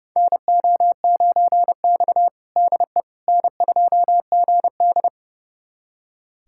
Morse code, heard as a keyed tone, about 30 wpm, 700 Hz. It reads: NO9X DE N2GD
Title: USA Callsign DE Callsign - 4 Characters - 30wpm